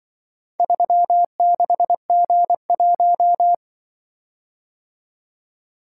Morse code, heard as a keyed tone, about 24 wpm, 700 Hz. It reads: 36G1
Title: Speed-Racing - Four Letter-Number Combinations - 20wpm